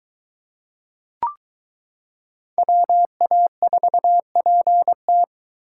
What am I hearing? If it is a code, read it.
Morse code, heard as a keyed tone, about 23 wpm, 700 Hz. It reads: WA4PT